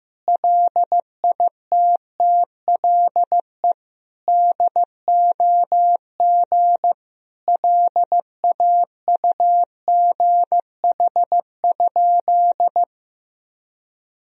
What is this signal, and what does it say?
Morse code, heard as a keyed tone, about 16 wpm, 700 Hz. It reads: LITTLE DOG LAUGH?
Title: Sentences from Top 500 Words and Abbreviations 15wpm